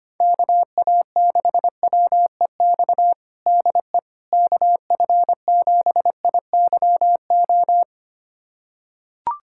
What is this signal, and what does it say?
Morse code, heard as a keyed tone, about 25 wpm, 700 Hz. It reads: KA6WEX DE KF7IYO
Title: USA Callsign DE Callsign - 6 Characters - 25wpm